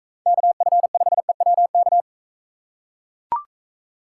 Morse code, heard as a keyed tone, about 42 wpm, 700 Hz. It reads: KF5EWK